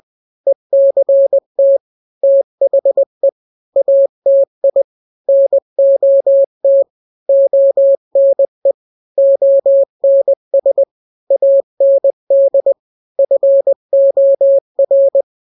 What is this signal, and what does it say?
Morse code, heard as a keyed tone, about 20 wpm, 550 Hz. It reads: ECT THE ATI NOT ONE ONS AND FOR
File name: n-grams / 3-grams 20 18 WPM